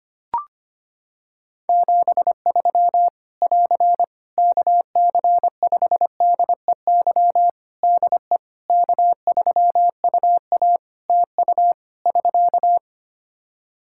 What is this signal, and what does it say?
Morse code, heard as a keyed tone, about 25 wpm, 700 Hz. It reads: 73 <AR> KC5DEY DE K3UA TU <SK>
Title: Practice QSO - Exchange 4 - Ending - 25wpm